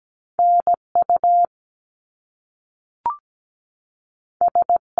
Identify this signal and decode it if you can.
Morse code, heard as a keyed tone, about 17 wpm, 700 Hz. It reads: NU S
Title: Top Two Letter Combinations in Words 17wpm